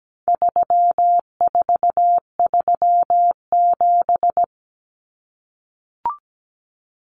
Morse code, heard as a keyed tone, about 17 wpm, 700 Hz. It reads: 3437
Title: Common Cut Numbers - 4-Digit 17wpm